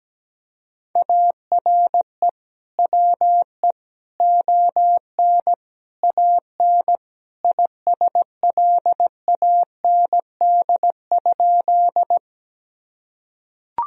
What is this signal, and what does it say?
Morse code, heard as a keyed tone, about 17 wpm, 700 Hz. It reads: ARE WE ON AN ISLAND?
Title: Sentences from Top 1000 Words - Review Part 3 - 17wpm